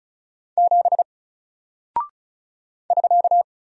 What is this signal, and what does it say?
Morse code, heard as a keyed tone, about 36 wpm, 700 Hz. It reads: 7 <SK>
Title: Single Character (Letter:Number:Punctuation:Prosign) Alt 35wpm